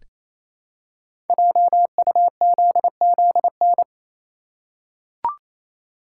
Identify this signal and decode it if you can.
Morse code, heard as a keyed tone, about 29 wpm, 700 Hz. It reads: JUZZN
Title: Code Groups - 5 Letters - 28wpm